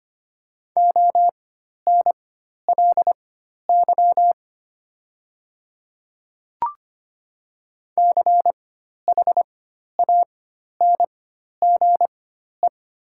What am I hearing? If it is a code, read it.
Morse code, heard as a keyed tone, about 25 wpm, spaced as for 12 wpm, 690 Hz. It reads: ONLY CHANGE
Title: 4x Character Spacing - Top 300 Words - 25wpm